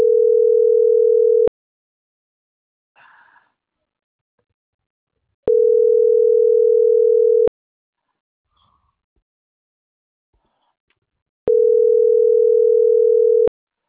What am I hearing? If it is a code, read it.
Ringback tone (North America), 440 + 480 Hz.